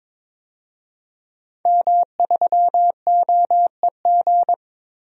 Morse code, heard as a keyed tone, about 22 wpm, 700 Hz. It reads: M3OEG
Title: DXCC-100 Call Signs - 5 Characters - 22wpm